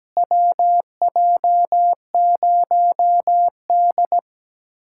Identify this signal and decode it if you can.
Morse code, heard as a keyed tone, about 17 wpm, 700 Hz. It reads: WJ0D
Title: USA Callsign DE Callsign - 4 Characters - 17wpm